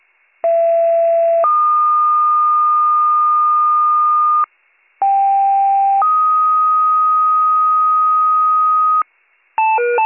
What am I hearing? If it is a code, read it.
Two-tone page, heard as a one-second tone, then a three-second tone, about 670 then 1220 Hz, about 790 then 1250 Hz. It is heard from 0.4 to 9.0 s.